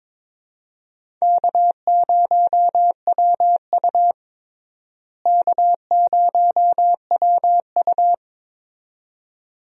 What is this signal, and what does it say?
Morse code, heard as a keyed tone, about 22 wpm, 700 Hz. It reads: K0WU K0WU
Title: USA Call Signs - 4 Characters - Repeated 2x - 22wpm